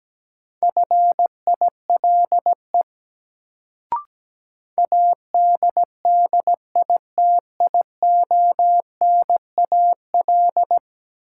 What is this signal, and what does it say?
Morse code, heard as a keyed tone, about 17 wpm, 700 Hz. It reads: FILE ADDITIONAL
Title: Sentences from Top 1000 Words - Review Part 4 - New Word Drill - 17wpm